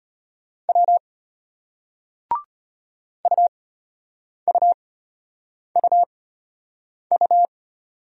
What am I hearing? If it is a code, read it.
Morse code, heard as a keyed tone, about 34 wpm, spaced as for 8 wpm, 700 Hz. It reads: W UUUU